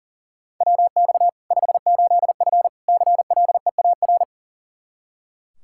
Morse code, heard as a keyed tone, about 37 wpm, 700 Hz. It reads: WX 58F CLEAR
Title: Practice QSO - Exchange 2 - Info - 40wpm